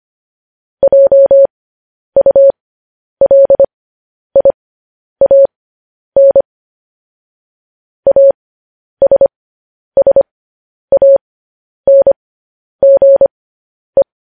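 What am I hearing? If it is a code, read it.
Morse code, heard as a keyed tone, about 25 wpm, spaced as for 10 wpm, 550 Hz. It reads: JULIAN ASSANGE
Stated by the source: Morse code news headlines sent at 10 wpm.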